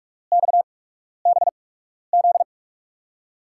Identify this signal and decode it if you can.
Morse code, heard as a keyed tone, about 41 wpm, spaced as for 12 wpm, 700 Hz. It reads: XBZ
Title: USA Call Signs - 5 Characters - 8x Character Spacing - 45wpm